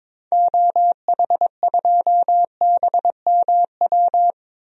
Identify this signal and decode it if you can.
Morse code, heard as a keyed tone, about 22 wpm, 700 Hz. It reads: OH2BMW
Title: DXCC-100 Call Signs - 6 Characters - Repeated 2x - 22wpm